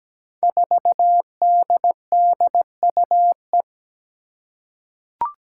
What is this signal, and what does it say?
Morse code, heard as a keyed tone, about 17 wpm, 700 Hz. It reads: 4DDUE
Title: Five Challenging Letter-Number Combinations 17wpm